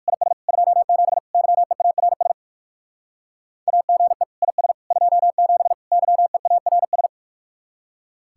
Morse code, heard as a keyed tone, about 53 wpm, 700 Hz. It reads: IS 27 YEARS AGE IS 27 YEARS
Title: Speed-Racing Common QSO Phrases - 45wpm